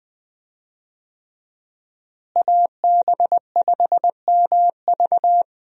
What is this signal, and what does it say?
Morse code, heard as a keyed tone, about 20 wpm, 700 Hz. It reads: AB5MV
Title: USA Call Signs - 5 Characters - Repeated 3x - 20wpm